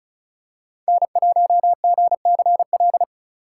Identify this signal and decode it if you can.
Morse code, heard as a keyed tone, about 36 wpm, 700 Hz. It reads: N1GCL
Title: USA Call Signs - 2x Character Spacing - 35wpm